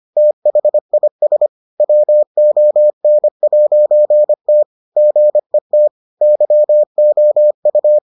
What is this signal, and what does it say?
Morse code, heard as a keyed tone, about 25 wpm, 600 Hz. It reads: THIS WON'T GET YOU